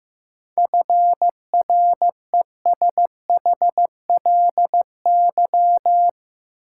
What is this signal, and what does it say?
Morse code, heard as a keyed tone, about 15 wpm, 700 Hz. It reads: FRESHLY
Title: Sets of 5 Words Encore - New Word Drill - 15wpm